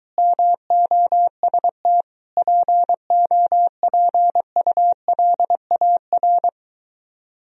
Morse code, heard as a keyed tone, about 22 wpm, 700 Hz. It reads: MOST POPULAR